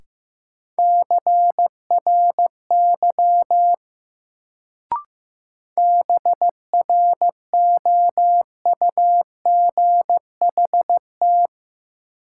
Morse code, heard as a keyed tone, about 15 wpm, 700 Hz. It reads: CRY BROUGHT
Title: Sentences from Top 500 Words - New Word Drill - 15wpm